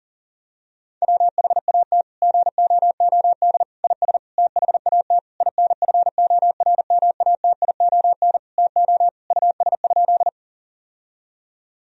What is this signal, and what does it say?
Morse code, heard as a keyed tone, about 37 wpm, 700 Hz. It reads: WHAT GOOD IS THAT INFORMATION TO US?
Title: Sentences from Top 1000 Words - Review Part 3 - 40wpm